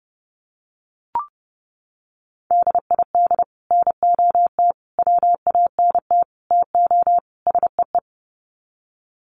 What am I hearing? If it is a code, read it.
Morse code, heard as a keyed tone, about 29 wpm, 700 Hz. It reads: DID NOT WANT TO SEE
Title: Sets of 5 Words Encore2 30wpm